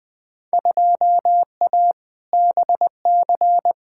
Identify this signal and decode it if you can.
Morse code, heard as a keyed tone, about 21 wpm, 700 Hz. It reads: 2A BC
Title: Field Day Exchanges - 20wpm